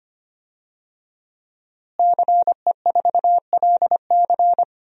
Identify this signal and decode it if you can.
Morse code, heard as a keyed tone, about 25 wpm, 700 Hz. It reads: CE4LC
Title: DXCC-100 Call Signs - 5 Characters - Repeated 2x - 25wpm